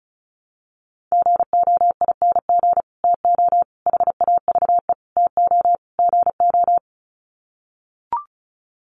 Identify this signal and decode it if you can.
Morse code, heard as a keyed tone, about 34 wpm, 700 Hz. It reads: GOING TO HAVE TO GO
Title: Sets of 5 Words Encore 35wpm